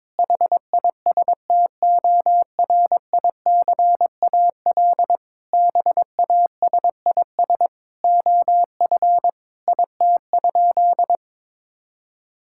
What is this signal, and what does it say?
Morse code, heard as a keyed tone, about 22 wpm, 700 Hz. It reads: HISTORICAL BASIS OF IT?
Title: Sentences from Top 1000 Words - Review Part 5 - 22wpm